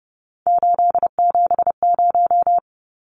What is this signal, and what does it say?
Morse code, heard as a keyed tone, about 30 wpm, 700 Hz. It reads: 870
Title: Numbers - 3-Digit 30wpm